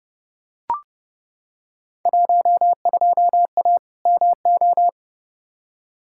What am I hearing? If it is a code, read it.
Morse code, heard as a keyed tone, about 30 wpm, 700 Hz. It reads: 12A MO
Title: Field Day Exchanges Encore - 30wpm